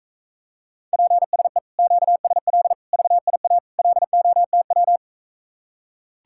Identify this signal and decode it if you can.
Morse code, heard as a keyed tone, about 44 wpm, 700 Hz. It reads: PSE QSL VIA LOTW